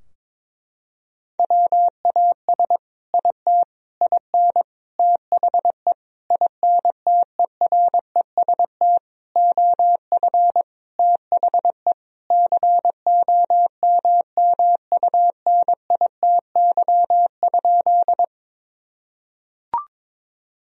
Morse code, heard as a keyed tone, about 22 wpm, 700 Hz. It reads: WAS IT IN THE INTEREST OF THE COMMUNITY?
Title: Sentences from Top 1000 Words - Review Part 4 - 22wpm